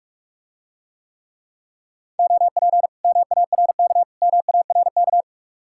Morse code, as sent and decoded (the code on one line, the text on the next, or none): --- .--. / -- .- .-. -.- / -- .- .-. -.-
OP MARK MARK